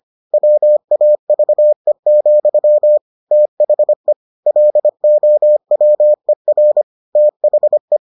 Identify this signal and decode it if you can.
Morse code, heard as a keyed tone, about 25 wpm, 600 Hz. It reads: WAVE, THE LOWER THE